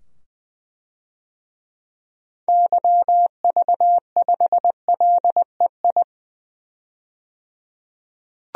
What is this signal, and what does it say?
Morse code, heard as a keyed tone, about 20 wpm, 700 Hz. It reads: YV5LEI
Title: DXCC-100 Call Signs - 6 Characters - 20wpm